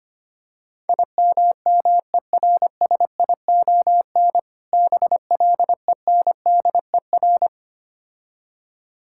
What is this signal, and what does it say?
Morse code, heard as a keyed tone, about 26 wpm, 700 Hz. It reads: IMMERSION BLENDER
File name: practice / 25wpm / Difficult Words - Speed-Racing with Hints - 25wpm